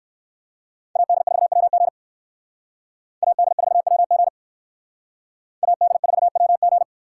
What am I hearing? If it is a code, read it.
Morse code, heard as a keyed tone, about 60 wpm, 700 Hz. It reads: AD4WG AD4WG AD4WG